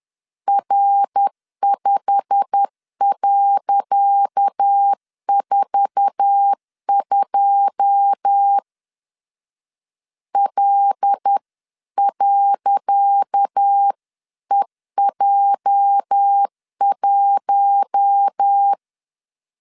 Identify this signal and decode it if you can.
Morse code, heard as a keyed tone, about 10 wpm, 800 Hz. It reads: R5.42 L.EJ1
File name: LEZIONI / 10 SIGNS / 10C DOT 10wpm 2940